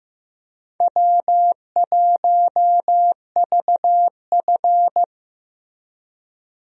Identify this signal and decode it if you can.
Morse code, heard as a keyed tone, about 15 wpm, 700 Hz. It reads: W1VF
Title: USA Call Signs - 4 Characters - Repeated 3x - 15wpm